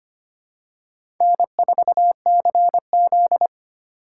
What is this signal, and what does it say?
Morse code, heard as a keyed tone, about 25 wpm, 700 Hz. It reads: N4CZ
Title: USA Call Signs - 4 Characters - 4x Character Spacing - 25wpm